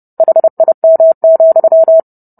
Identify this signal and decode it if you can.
Morse code, heard as a keyed tone, about 30 wpm, 650 Hz. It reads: HIM,